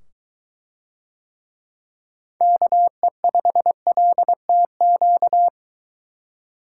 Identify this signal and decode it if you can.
Morse code, heard as a keyed tone, about 22 wpm, 700 Hz. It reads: KE5LTQ